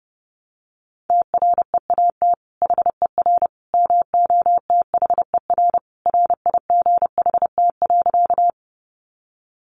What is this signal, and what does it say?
Morse code, heard as a keyed tone, about 30 wpm, 700 Hz. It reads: TREAT HER MOTHER RIGHT.
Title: Sentences from Top 1300 Words 30wpm